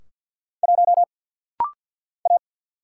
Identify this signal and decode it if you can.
Morse code, heard as a keyed tone, about 49 wpm, 700 Hz. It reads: 1 A
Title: Single Letter-Number - Rapid-Fire 50wpm